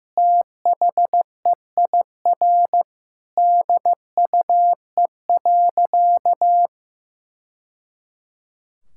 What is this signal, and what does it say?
Morse code, heard as a keyed tone, about 15 wpm, 700 Hz. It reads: THEIR DUE.
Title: Sentences from Top 1000 Words - Review Part 6 - 15wpm